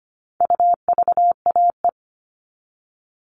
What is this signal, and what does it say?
Morse code, heard as a keyed tone, about 25 wpm, 700 Hz. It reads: UVAE
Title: All Cut Numbers - 4-Digit 25wpm